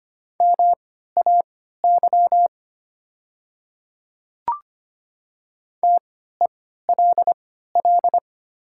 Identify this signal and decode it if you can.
Morse code, heard as a keyed tone, about 25 wpm, spaced as for 14 wpm, 700 Hz. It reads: MAY TELL
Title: 3x Character Spacing - Top 200 Words - 25wpm